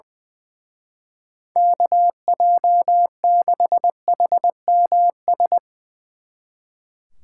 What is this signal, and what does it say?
Morse code, heard as a keyed tone, about 20 wpm, 700 Hz. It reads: KJ6HMS
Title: Speed Racing - USA 2x3 Call Signs 20wpm